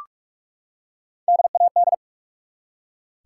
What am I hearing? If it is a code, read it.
Morse code, heard as a keyed tone, about 42 wpm, 700 Hz. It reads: DAD
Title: Three Letter Words - 45wpm